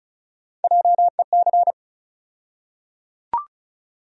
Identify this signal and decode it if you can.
Morse code, heard as a keyed tone, about 34 wpm, 700 Hz. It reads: JEC
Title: USA Call Sign Postfixes - 3 Letters 35wpm